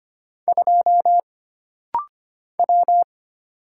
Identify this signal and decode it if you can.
Morse code, heard as a keyed tone, about 25 wpm, 700 Hz. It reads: 2 W
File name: Single Character (Letter:Number:Punctuation:Prosign) - Rapid-Fire - 25wpm